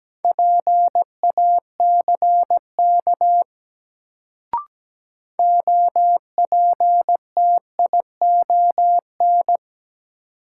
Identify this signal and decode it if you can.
Morse code, heard as a keyed tone, about 17 wpm, 700 Hz. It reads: PACK OPTION
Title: Sentences from Top 1300 Words - New Word Drill - 17wpm